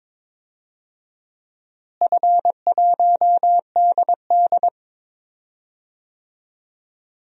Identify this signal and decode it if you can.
Morse code, heard as a keyed tone, about 22 wpm, 700 Hz. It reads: F1DD
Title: DXCC-100 Call Signs - 4 Characters - Repeated 2x - 22wpm